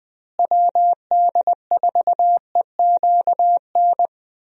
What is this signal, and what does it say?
Morse code, heard as a keyed tone, about 20 wpm, 700 Hz. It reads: WD4EQN
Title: USA Call Signs - 6 Characters - Repeated 3x - 20wpm